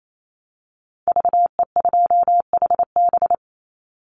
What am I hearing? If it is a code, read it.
Morse code, heard as a keyed tone, about 28 wpm, 700 Hz. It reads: VE2HB
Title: Call Signs - 28wpm